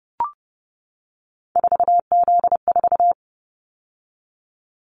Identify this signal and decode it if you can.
Morse code, heard as a keyed tone, about 30 wpm, 700 Hz. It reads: 4Z4
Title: Callsign Prefixes for Top 50 DXCC - 30wpm